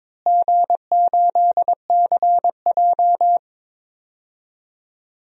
Morse code, heard as a keyed tone, about 22 wpm, 700 Hz. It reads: G8CJ
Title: DXCC-100 Call Signs - 4 Characters - Repeated 2x - 22wpm